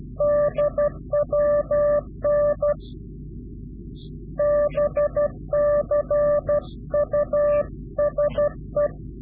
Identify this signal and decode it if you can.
Morse code, heard as a keyed tone, about 12 wpm, 590 Hz. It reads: DWN BCUSE
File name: CW-QSO / Directional Call / Directional QSO